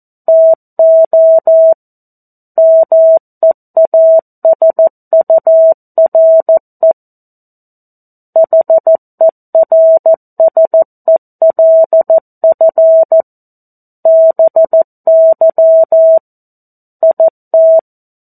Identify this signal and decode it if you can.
Morse code, heard as a keyed tone, about 14 wpm, 650 Hz. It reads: TO MEASURE HERSELF BY IT
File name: 02-Chapter-02 14 WPM